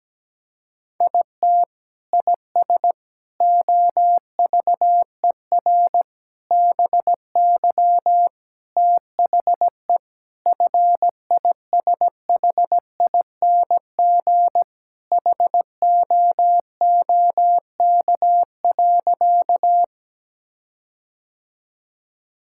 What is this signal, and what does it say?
Morse code, heard as a keyed tone, about 17 wpm, 700 Hz. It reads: IT IS OVER BY THE FISHING HOOK.
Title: Sentences from Top 1600 Words 17wpm